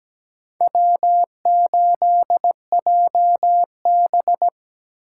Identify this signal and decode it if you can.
Morse code, heard as a keyed tone, about 17 wpm, 700 Hz. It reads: W8JB
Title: USA Call Signs - 4 Characters - 3x Character Spacing - 17wpm